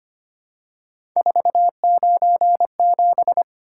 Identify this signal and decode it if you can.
Morse code, heard as a keyed tone, about 25 wpm, 700 Hz. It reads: 497
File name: Three Letter-Number Combinations - 25wpm